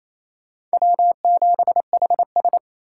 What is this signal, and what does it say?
Morse code, heard as a keyed tone, about 33 wpm, 700 Hz. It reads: W7HS
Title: Speed Racing - USA 1x2 Call Signs 28wpm